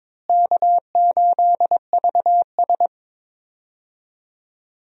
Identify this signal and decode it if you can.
Morse code, heard as a keyed tone, about 22 wpm, 700 Hz. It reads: K8VS
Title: USA Call Signs - 4 Characters - Repeated 3x - 22wpm